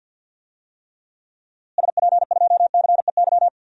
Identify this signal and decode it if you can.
Morse code, heard as a keyed tone, about 51 wpm, 700 Hz. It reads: SP2CEY
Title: DXCC-100 Call Signs - 6 Characters - Repeated 3x - 50wpm